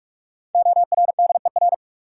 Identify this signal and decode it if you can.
Morse code, heard as a keyed tone, about 46 wpm, 700 Hz. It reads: ORDER
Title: Sentences from Top 400 Words - New Word Drill - 45wpm